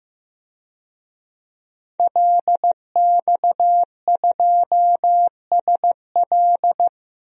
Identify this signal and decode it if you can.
Morse code, heard as a keyed tone, about 15 wpm, 700 Hz. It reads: LX2SL
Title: DXCC-100 Call Signs - 5 Characters - Repeated 3x - 15wpm